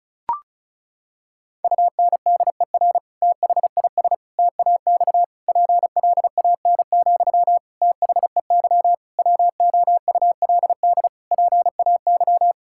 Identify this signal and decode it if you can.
Morse code, heard as a keyed tone, about 36 wpm, 700 Hz. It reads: UNDER THIS TAX PLAN, THEY WOULD PAY
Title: Sentences from Top 1000 Words - Review Part 5 - 35wpm